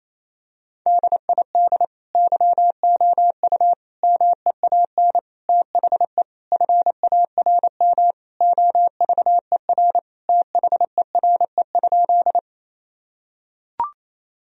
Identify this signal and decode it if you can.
Morse code, heard as a keyed tone, about 28 wpm, 700 Hz. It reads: DID YOU MEAN THE FARM OVER THERE?
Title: Sentences from Top 1000 Words - Review Part 2 - 28wpm